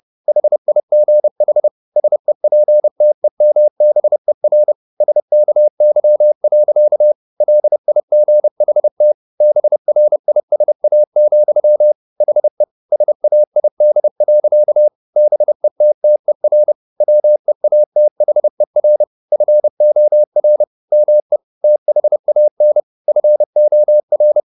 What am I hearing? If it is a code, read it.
Morse code, heard as a keyed tone, about 30 wpm, 600 Hz. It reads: HIGH SEPTEMBER SKY. LIGHT BRISA, HE SAID. BETTER WEATHER FOR ME THAN FOR
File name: the old man and the sea / the old man and the sea 1 30wpm